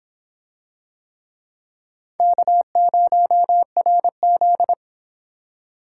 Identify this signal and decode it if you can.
Morse code, heard as a keyed tone, about 26 wpm, 700 Hz. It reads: K0RZ